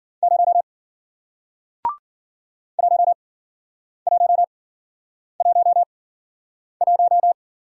Morse code, heard as a keyed tone, about 53 wpm, spaced as for 9 wpm, 700 Hz. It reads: 0 1111